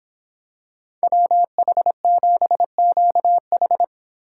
Morse code, heard as a keyed tone, about 26 wpm, 700 Hz. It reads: WH7QH